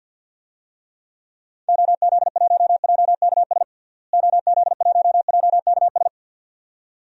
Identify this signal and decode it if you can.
Morse code, heard as a keyed tone, about 47 wpm, 700 Hz. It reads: OZ1JKS OZ1JKS
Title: DXCC-100 Call Signs - 6 Characters - Repeated 2x - 50wpm